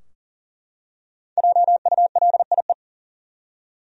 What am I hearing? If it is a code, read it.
Morse code, heard as a keyed tone, about 40 wpm, 700 Hz. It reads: JULIE
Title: USA Common Names - 40wpm